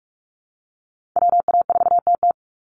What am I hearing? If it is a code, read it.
Morse code, heard as a keyed tone, about 44 wpm, 700 Hz. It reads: WA4TT
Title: Call Signs Repeated 2x - 45wpm